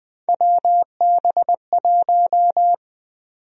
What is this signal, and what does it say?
Morse code, heard as a keyed tone, about 20 wpm, 700 Hz. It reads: WB1
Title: USA Call Sign Prefixes - 20wpm